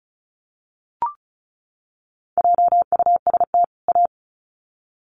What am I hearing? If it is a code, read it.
Morse code, heard as a keyed tone, about 34 wpm, 700 Hz. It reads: JUST A
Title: Sets of 2 Words 35wpm